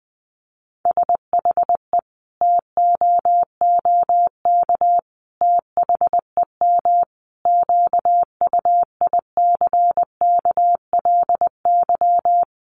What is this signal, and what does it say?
Morse code, heard as a keyed tone, about 20 wpm, 700 Hz. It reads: SHE TOOK THEM QUICKLY
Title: Sentences from Top 1300 Words 20wpm